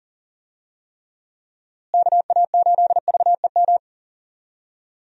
Morse code, heard as a keyed tone, about 37 wpm, 700 Hz. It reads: KA8VEM
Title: USA Call Signs - 6 Characters - Repeated 2x - 40wpm